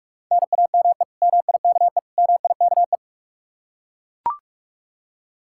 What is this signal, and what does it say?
Morse code, heard as a keyed tone, about 41 wpm, 700 Hz. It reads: NAME MIKE MIKE
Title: Practice QSO - Exchange 1 - Beginning - 45wpm